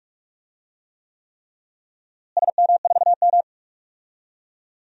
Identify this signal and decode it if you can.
Morse code, heard as a keyed tone, about 42 wpm, 700 Hz. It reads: SM4M
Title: DXCC-100 Call Signs - 4 Characters - Repeated 3x - 45wpm